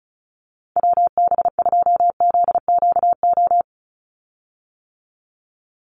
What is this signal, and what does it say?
Morse code, heard as a keyed tone, about 34 wpm, 700 Hz. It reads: WB2ZQO
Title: USA Call Signs 35wpm